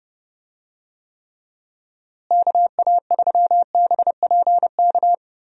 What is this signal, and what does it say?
Morse code, heard as a keyed tone, about 32 wpm, 700 Hz. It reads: KA3BPK
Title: USA Call Signs - 6 Characters - Repeated 3x - 30wpm